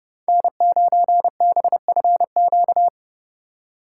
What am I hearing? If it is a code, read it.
Morse code, heard as a keyed tone, about 28 wpm, 700 Hz. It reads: N9BFQ